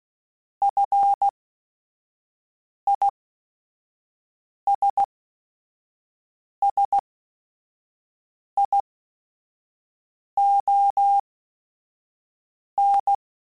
Morse code, heard as a keyed tone, about 16 wpm, spaced as for 5 wpm, 800 Hz. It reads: FISSION